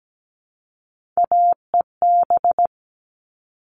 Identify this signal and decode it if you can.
Morse code, heard as a keyed tone, about 17 wpm, 700 Hz. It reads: AEB
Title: All Cut Numbers - 3-Digit 17wpm